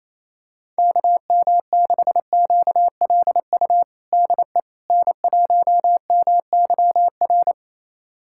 Morse code, heard as a keyed tone, about 28 wpm, 700 Hz. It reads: KM6QLU DE N1MYR
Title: USA Callsign DE Callsign 28wpm